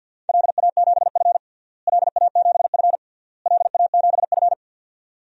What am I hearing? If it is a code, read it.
Morse code, heard as a keyed tone, about 51 wpm, 700 Hz. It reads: LA7F LA7F LA7F
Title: DXCC-100 Call Signs - 4 Characters - Repeated 3x - 50wpm